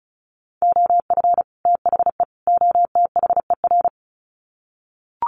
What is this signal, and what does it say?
Morse code, heard as a keyed tone, about 34 wpm, 700 Hz. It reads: OF THE OTHER
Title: Sets of 3 Words 35wpm